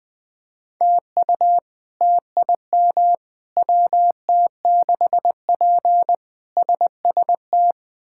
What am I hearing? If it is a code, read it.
Morse code, heard as a keyed tone, about 20 wpm, 700 Hz. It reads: TU TIM WT6P SST
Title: K1USN SST QSO Practice - 20wpm